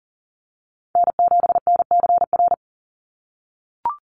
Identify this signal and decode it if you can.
Morse code, heard as a keyed tone, about 40 wpm, 700 Hz. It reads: N7NCR
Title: USA Call Signs 40wpm